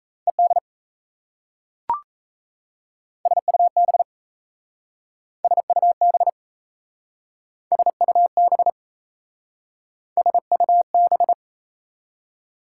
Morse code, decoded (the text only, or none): ED SUB SUB SUB SUB